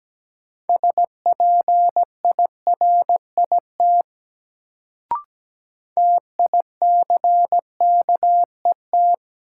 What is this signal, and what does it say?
Morse code, heard as a keyed tone, about 17 wpm, 700 Hz. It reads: SPIRIT TICKET
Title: Sentences from Top 1400 Words - New Word Drill - 17wpm